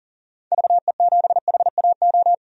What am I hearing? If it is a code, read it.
Morse code, heard as a keyed tone, about 40 wpm, 700 Hz. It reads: VE7HAO
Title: DXCC-100 Call Signs - 6 Characters - 40wpm